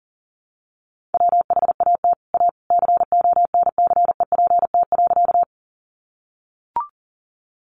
Morse code, heard as a keyed tone, about 37 wpm, 700 Hz. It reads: WHAT A CONCEPT.
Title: Sentences from Top 1100 Words 40wpm